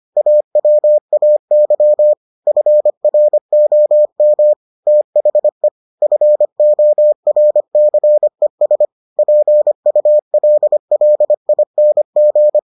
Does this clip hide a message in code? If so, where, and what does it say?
Yes: Morse code from 0.2 to 12.6 s, heard as a keyed tone, about 25 wpm, 600 Hz. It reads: AWAY FROM THE FORCES PULLING